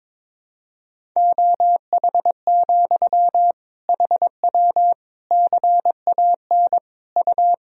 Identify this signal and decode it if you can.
Morse code, heard as a keyed tone, about 21 wpm, 700 Hz. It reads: OH, HW CAN U
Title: Sentences from Top 500 Words and Abbreviations 22wpm